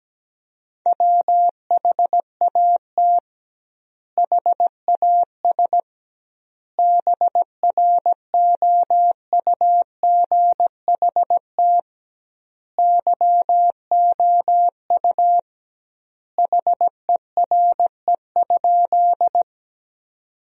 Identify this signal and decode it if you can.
Morse code, heard as a keyed tone, about 17 wpm, 700 Hz. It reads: WHAT HAS BROUGHT YOU HERE?
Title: 2x Word-Spacing - Sentences from Top 500 Words - 17wpm